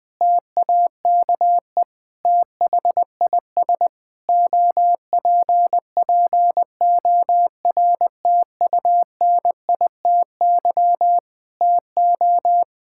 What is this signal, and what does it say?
Morse code, heard as a keyed tone, about 20 wpm, 700 Hz. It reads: TAKE THIS OPPORTUNITY TO
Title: Sets of 8 Words - 20wpm